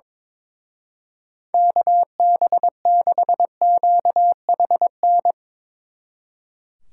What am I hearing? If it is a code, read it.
Morse code, heard as a keyed tone, about 24 wpm, 700 Hz. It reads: KB6QHN
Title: Speed Racing - USA 2x3 Call Signs 22wpm